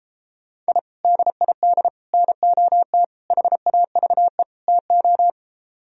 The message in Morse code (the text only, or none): I DID NOT HAVE TO